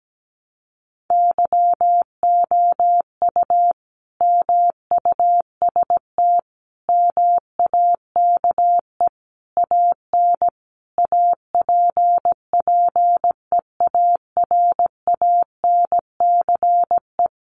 Morse code, heard as a keyed tone, about 17 wpm, 700 Hz. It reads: YOU MUST MAKE AN APPEARANCE
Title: Sentences from Top 1100 Words 17wpm